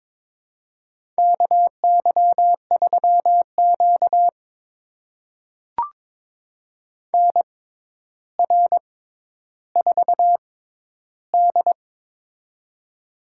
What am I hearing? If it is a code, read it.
Morse code, heard as a keyed tone, about 23 wpm, spaced as for 11 wpm, 700 Hz. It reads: KY3Q NR4D
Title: USA Call Signs - 4 Characters - 6x Character Spacing - 22wpm